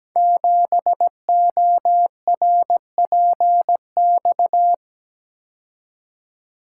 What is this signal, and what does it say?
Morse code, heard as a keyed tone, about 17 wpm, 700 Hz. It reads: 7ORPX
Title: Code Groups - 5 Letter:Numbers - 17wpm